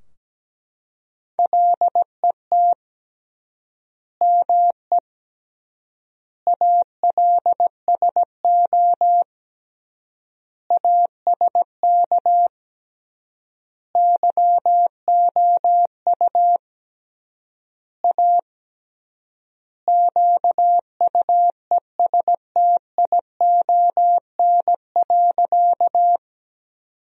Morse code, heard as a keyed tone, about 17 wpm, 700 Hz. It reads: LET ME ALSO ASK YOU A QUESTION.
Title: 3x Word-Spacing - Sentences from Top 400 Words - 17wpm